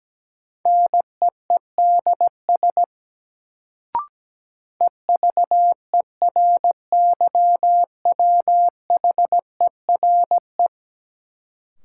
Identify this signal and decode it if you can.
Morse code, heard as a keyed tone, about 17 wpm, 700 Hz. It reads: NEEDS EVERYWHERE
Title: Sentences from Top 1600 Words - New Word Drill Alt - 17wpm